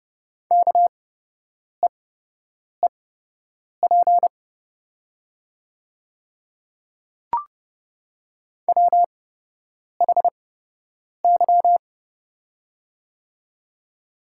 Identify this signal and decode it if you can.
Morse code, heard as a keyed tone, about 30 wpm, spaced as for 8 wpm, 700 Hz. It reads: KEEP WHY
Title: 8x Character Spacing - Top 300 Words - 30wpm